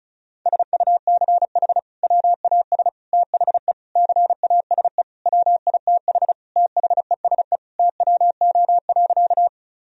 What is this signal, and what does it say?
Morse code, heard as a keyed tone, about 36 wpm, 700 Hz. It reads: SUCH WAS THE CASE WITH THESE TWO.
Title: Sentences from Top 1000 Words - Review Part 1 - 35wpm